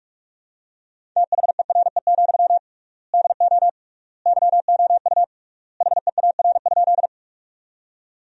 Morse code, heard as a keyed tone, about 43 wpm, 700 Hz. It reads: THERE, DO YOU HEAR?
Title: 3x Word-Spacing - Sentences from Top 300 Words - 45wpm